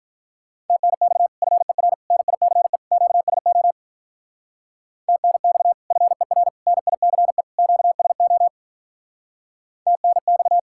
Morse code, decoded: TNX FER NICE QSO TNX FER NICE QSO TNX

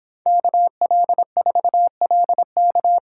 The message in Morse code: KL4LK

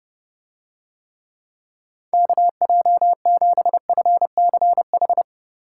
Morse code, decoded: KJ7FCH

